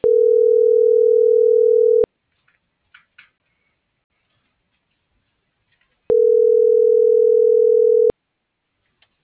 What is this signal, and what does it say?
Ringback tone (North America), 440 + 480 Hz.